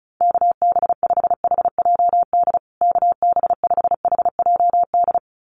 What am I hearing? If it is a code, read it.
Morse code, heard as a keyed tone, about 34 wpm, 700 Hz. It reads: KB5HJD KB5HJD
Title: USA Call Signs Repeated 2x 35wpm